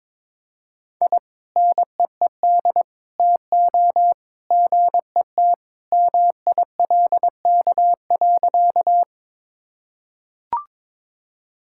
Morse code, heard as a keyed tone, about 22 wpm, 700 Hz. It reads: I NEED TO GET MILK.
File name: practice / 22wpm / Sentences with 2-Word Phrases - 22wpm